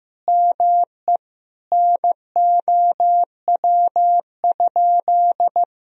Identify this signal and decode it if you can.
Morse code, heard as a keyed tone, about 16 wpm, 700 Hz. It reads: ME NOW?
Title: Sentences from Top 1000 Words - Review Part 3 - 15wpm